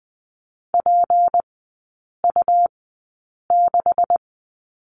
Morse code, heard as a keyed tone, about 20 wpm, spaced as for 8 wpm, 700 Hz. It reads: PU6
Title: 152 Lesson 25e - Letter p - Three Characters 20wpm